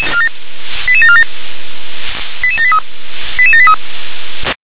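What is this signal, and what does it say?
Five-tone selective call (ZVEI), heard as quick runs of five tones, tones of about 70 ms. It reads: 88747 88783 88783
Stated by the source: ZVEI five-tone call 88911 received by SDR.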